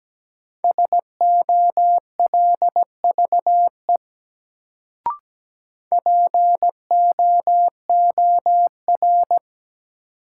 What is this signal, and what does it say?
Morse code, heard as a keyed tone, about 17 wpm, 700 Hz. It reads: SOLVE POOR
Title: Sentences from Top 1000 Words - New Word Drill - 17wpm